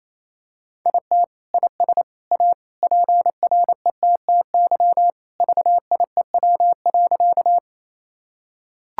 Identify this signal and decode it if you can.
Morse code, heard as a keyed tone, about 28 wpm, 700 Hz. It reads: IT IS A PRETTY VIEW.
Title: Sentences from Top 1000 Words - Review Part 6 - 28wpm